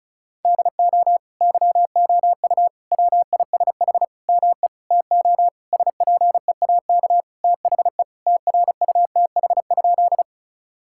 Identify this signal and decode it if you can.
Morse code, heard as a keyed tone, about 36 wpm, 700 Hz. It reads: DO YOU WISH ME TO SPEAK THE TRUTH?
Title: Sentences from Top 1600 Words 35wpm